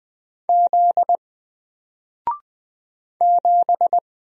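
Morse code, heard as a keyed tone, about 20 wpm, 700 Hz. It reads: Z 7
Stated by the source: Single letter and number Morse drill at 20 wpm, from N4LCD.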